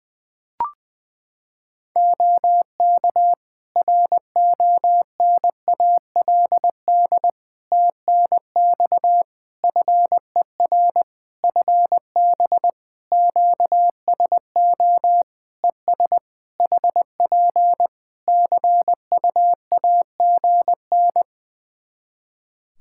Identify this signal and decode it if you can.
Morse code, heard as a keyed tone, about 20 wpm, 700 Hz. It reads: OK RONALD TNX FER FB QSO ES HP CUAGN
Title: Practice QSO - Exchange 4 - Ending - 20wpm